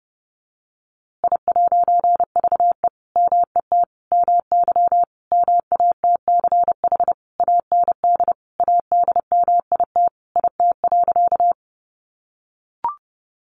Morse code, heard as a keyed tone, about 28 wpm, 700 Hz. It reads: I'VE MET MY MATCH AND ADMIT IT.
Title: A Sound Of Thunder 30wpm